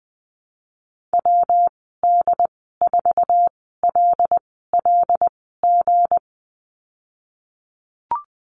Morse code, heard as a keyed tone, about 19 wpm, spaced as for 14 wpm, 700 Hz. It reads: WD4LLG
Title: 179 Lesson 29g - Slash - Call Signs 20wpm